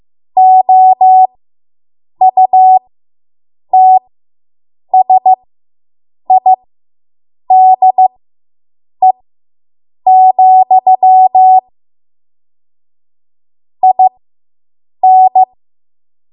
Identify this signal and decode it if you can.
Morse code, heard as a keyed tone, about 15 wpm, spaced as for 7 wpm, 750 Hz. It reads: OUTSIDE, IN